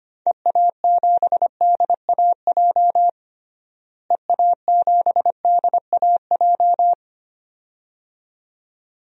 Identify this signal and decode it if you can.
Morse code, heard as a keyed tone, about 25 wpm, 700 Hz. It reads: EA7DAJ EA7DAJ
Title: DXCC-100 Call Signs - 6 Characters - Repeated 2x - 25wpm